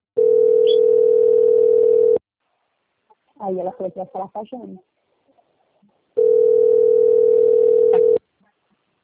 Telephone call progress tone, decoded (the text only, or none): ringback tone